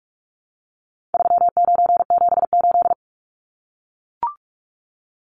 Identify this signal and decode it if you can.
Morse code, heard as a keyed tone, about 44 wpm, 700 Hz. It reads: 3978
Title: Numbers - 4-Digit 45wpm